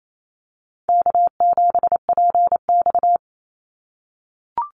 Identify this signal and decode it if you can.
Morse code, heard as a keyed tone, about 28 wpm, 700 Hz. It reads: K7PX
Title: NA SOTA Activator Callsigns 28wpm